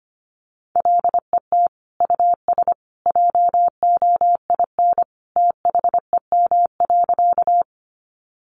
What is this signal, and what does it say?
Morse code, heard as a keyed tone, about 25 wpm, 700 Hz. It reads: LET US JOIN THEM.
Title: Sentences from Top 1400 Words 25wpm